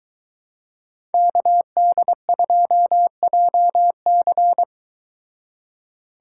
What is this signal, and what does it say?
Morse code, heard as a keyed tone, about 22 wpm, 700 Hz. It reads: KD2JC